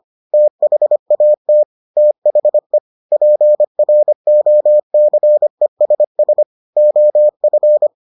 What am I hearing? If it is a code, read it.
Morse code, heard as a keyed tone, about 25 wpm, 600 Hz. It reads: THAT THE PROCESS OF